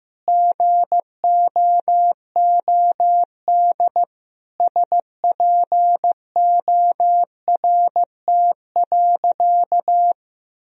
Morse code, heard as a keyed tone, about 15 wpm, 700 Hz. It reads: GOOD SPORT.
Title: Sentences from Top 1000 Words - Review Part 4 - 15wpm